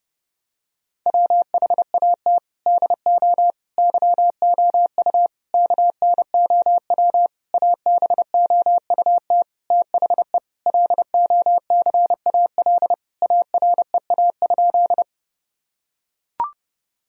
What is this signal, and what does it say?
Morse code, heard as a keyed tone, about 31 wpm, 700 Hz. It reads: WHAT DO YOU KNOW ABOUT THE LOCAL AREA?
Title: Sentences from Top 1000 Words - Review Part 3 - 30wpm